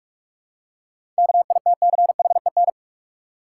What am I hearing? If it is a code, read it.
Morse code, heard as a keyed tone, about 43 wpm, 700 Hz. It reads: KITCHEN
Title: Seven Letter Words - 45wpm